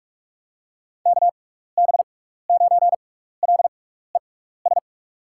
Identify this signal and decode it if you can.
Morse code, heard as a keyed tone, about 46 wpm, spaced as for 16 wpm, 700 Hz. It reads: KB9LES
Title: USA Call Signs - 6 Characters - 6x Character Spacing - 45wpm